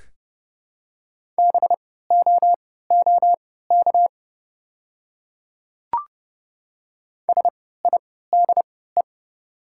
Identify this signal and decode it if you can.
Morse code, heard as a keyed tone, about 31 wpm, spaced as for 17 wpm, 700 Hz. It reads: BOOK SIDE
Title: 3x Character Spacing - Top 300 Words - 30wpm